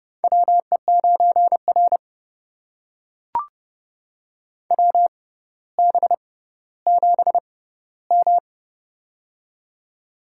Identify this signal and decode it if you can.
Morse code, heard as a keyed tone, about 30 wpm, spaced as for 15 wpm, 700 Hz. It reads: WE9R WB7M
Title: USA Call Signs - 4 Characters - 6x Character Spacing - 30wpm